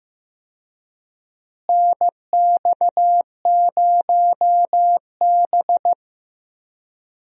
Morse code, heard as a keyed tone, about 15 wpm, 700 Hz. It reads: NX0B
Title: USA Call Signs - 4 Characters - Repeated 3x - 15wpm